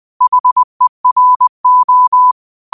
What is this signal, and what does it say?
Morse code, heard as a keyed tone, about 20 wpm, 1000 Hz. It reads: HERO
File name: qotdfiles / 20wpm